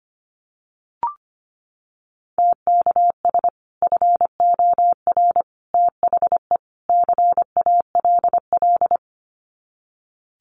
Morse code, heard as a keyed tone, about 25 wpm, 700 Hz. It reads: TKS FOR THE CALL
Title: Common QSO Phrases 25wpm